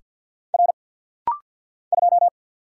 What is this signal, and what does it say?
Morse code, heard as a keyed tone, about 48 wpm, 700 Hz. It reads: R 2
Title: Single Letter-Number - Rapid-Fire 50wpm